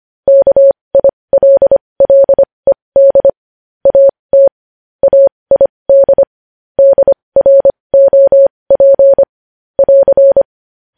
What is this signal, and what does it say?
Morse code, heard as a keyed tone, about 25 wpm, spaced as for 20 wpm, 550 Hz. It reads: KILLED AT AID DROP <AR>
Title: News Headlines at 20